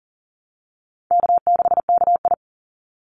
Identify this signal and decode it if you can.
Morse code, heard as a keyed tone, about 40 wpm, 700 Hz. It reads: K6KI
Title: Easy Call Signs Repeated 3x - 40wpm